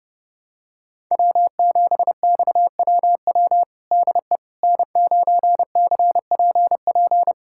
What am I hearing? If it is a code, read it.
Morse code, heard as a keyed tone, about 30 wpm, 700 Hz. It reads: W7XWW DE N9CPP
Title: USA Callsign DE Callsign - 5 Characters - 30wpm